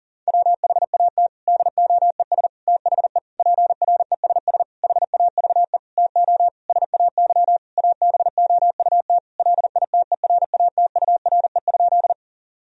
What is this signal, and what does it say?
Morse code, heard as a keyed tone, about 40 wpm, 700 Hz. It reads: WHAT DOES THE PRESS HAVE TO SAY ABOUT LITERATURE?
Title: Sentences from Top 1000 Words - Review Part 3 - 40wpm